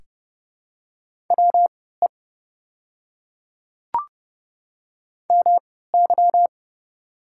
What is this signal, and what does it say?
Morse code, heard as a keyed tone, about 30 wpm, spaced as for 17 wpm, 700 Hz. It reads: WE MY
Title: 3x Character Spacing - Top 100 Words - 30wpm